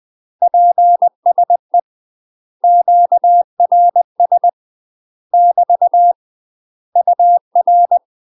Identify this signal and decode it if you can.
Morse code, heard as a keyed tone, about 20 wpm, 700 Hz. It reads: PSE QRS <BT> UR
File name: UK MASTER QSOs TRAINING / 20WPM / UK MASTER QSOs 0053